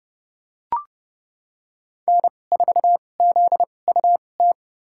Morse code, heard as a keyed tone, about 30 wpm, spaced as for 22 wpm, 700 Hz. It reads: N4ZUT
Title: USA Call Signs - 5 Characters - 2x Character Spacing - 30wpm